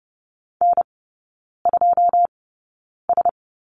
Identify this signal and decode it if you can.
Morse code, heard as a keyed tone, about 30 wpm, spaced as for 9 wpm, 700 Hz. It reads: N2S
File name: Lesson 19e - Letter w - Three Characters 30wpm